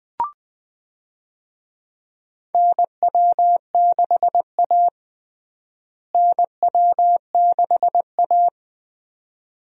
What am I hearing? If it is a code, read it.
Morse code, heard as a keyed tone, about 20 wpm, 700 Hz. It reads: NW6A NW6A
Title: USA Call Signs - 4 Characters - Repeated 2x - 20wpm